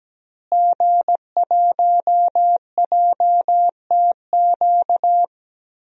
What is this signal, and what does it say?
Morse code, heard as a keyed tone, about 17 wpm, 700 Hz. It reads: G1JTQ
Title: DXCC-100 Call Signs - 5 Characters - Repeated 3x - 17wpm